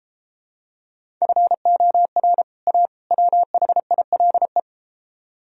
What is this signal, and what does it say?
Morse code, heard as a keyed tone, about 33 wpm, 700 Hz. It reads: FOR A WHILE